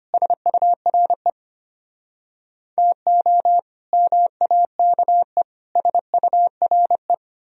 Morse code, heard as a keyed tone, about 32 wpm, 700 Hz. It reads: SURE TO MAKE SURE